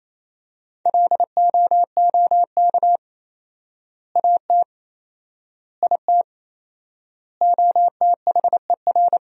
Morse code, heard as a keyed tone, about 28 wpm, 700 Hz. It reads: LOOK AT IT OTHER
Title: 4x Word-Spacing - Sentences from Top 200 Words - 28wpm